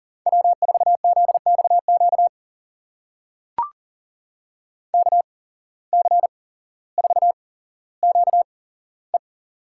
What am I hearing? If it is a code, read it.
Morse code, heard as a keyed tone, about 40 wpm, spaced as for 17 wpm, 700 Hz. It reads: W4ZXQ KC4QE
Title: USA Call Signs - 5 Characters - 8x Character Spacing - 40wpm